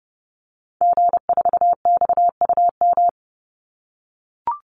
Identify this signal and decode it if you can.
Morse code, heard as a keyed tone, about 30 wpm, 700 Hz. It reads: G4XUM
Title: Call Signs - 30wpm